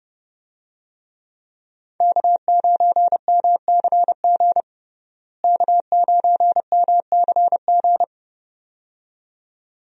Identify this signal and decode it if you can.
Morse code, heard as a keyed tone, about 30 wpm, 700 Hz. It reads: K9MCG K9MCG
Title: USA Call Signs - 5 Characters - Repeated 2x - 30wpm